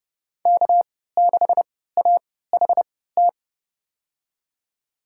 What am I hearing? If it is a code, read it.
Morse code, heard as a keyed tone, about 30 wpm, spaced as for 17 wpm, 700 Hz. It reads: K6AHT
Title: USA Call Signs - 5 Characters - 3x Character Spacing - 30wpm